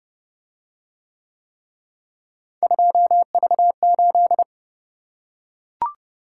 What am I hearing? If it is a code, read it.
Morse code, heard as a keyed tone, about 29 wpm, 700 Hz. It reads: 2V8